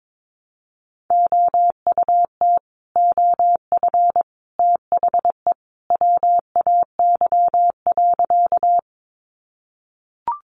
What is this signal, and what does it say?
Morse code, heard as a keyed tone, about 22 wpm, 700 Hz. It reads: OUT OF THE WAY.
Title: Sentences from Top 100 Words 22wpm